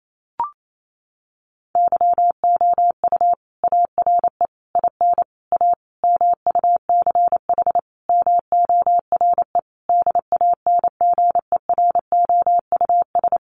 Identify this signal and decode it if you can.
Morse code, heard as a keyed tone, about 28 wpm, 700 Hz. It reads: YOU ARE IN A MUCH MORE DANGEROUS
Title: Sentences from Top 1400 Words 28wpm